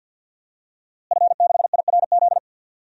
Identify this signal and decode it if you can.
Morse code, heard as a keyed tone, about 49 wpm, 700 Hz. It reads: F6IRZ
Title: DXCC-100 Call Signs - 5 Characters - Repeated 2x - 50wpm